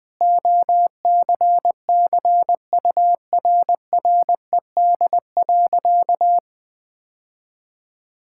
Morse code, heard as a keyed tone, about 20 wpm, 700 Hz. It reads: OCCURRED.
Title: Sentences with 6-Word Phrases - 20wpm